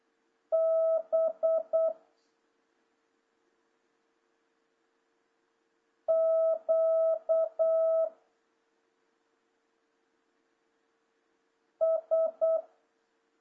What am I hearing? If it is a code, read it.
Morse code, heard as a keyed tone, about 8 wpm, 650 Hz. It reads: B Q S
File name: jh8cbh / 2026cw